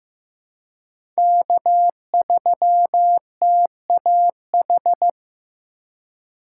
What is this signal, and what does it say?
Morse code, heard as a keyed tone, about 15 wpm, 700 Hz. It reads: K3TAH
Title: USA Call Signs - 3x Character Spacing - 15wpm